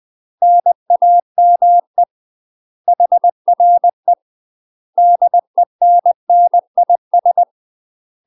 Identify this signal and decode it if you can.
Morse code, heard as a keyed tone, about 20 wpm, 700 Hz. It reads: NAME HRE DENNIS
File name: UK MASTER QSOs TRAINING / 20WPM / UK MASTER QSOs 0029